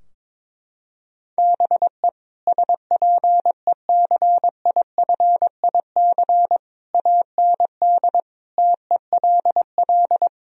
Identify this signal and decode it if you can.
Morse code, heard as a keyed tone, about 22 wpm, 700 Hz. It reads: BE SPECIFIC AND TELL
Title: Sentences from Top 1000 Words - Review Part 3 - 22wpm